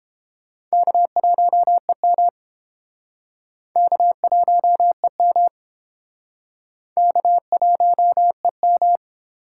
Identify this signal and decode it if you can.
Morse code, heard as a keyed tone, about 32 wpm, 700 Hz. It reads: K1EM K1EM K1EM